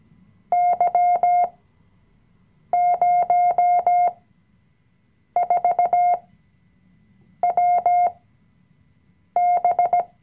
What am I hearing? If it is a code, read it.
Morse code, heard as a keyed tone, about 17 wpm, spaced as for 6 wpm, 710 Hz. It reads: Y04WB
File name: CW-Online-Kurs / Anfänger / Lektion 020